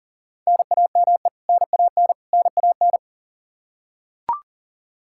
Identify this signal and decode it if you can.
Morse code, heard as a keyed tone, about 40 wpm, 700 Hz. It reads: NAME NAN NAN
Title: Practice QSO - Exchange 1 - Beginning - 40wpm